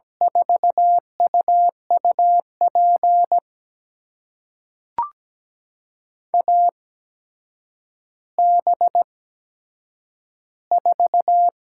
Morse code, heard as a keyed tone, about 17 wpm, spaced as for 8 wpm, 700 Hz. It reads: 4UUP AB4
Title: USA Call Signs - 8x Character Spacing - 17wpm